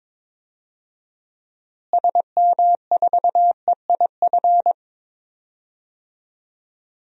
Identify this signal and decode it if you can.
Morse code, heard as a keyed tone, about 22 wpm, 700 Hz. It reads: SM4EIF
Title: DXCC-100 Call Signs - 6 Characters - Repeated 3x - 22wpm